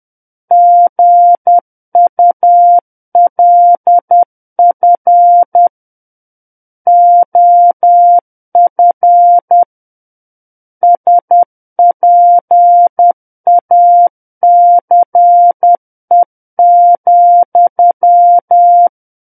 Morse code, heard as a keyed tone, about 10 wpm, 700 Hz. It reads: GULF OF SPACE,
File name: Book One / 01-Chapter 1 - 10 WPM